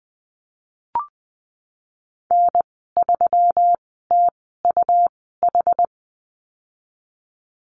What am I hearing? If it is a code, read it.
Morse code, heard as a keyed tone, about 20 wpm, spaced as for 15 wpm, 700 Hz. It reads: N3TUH
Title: 161 Lesson 26g - Letter g - Call Signs 20wpm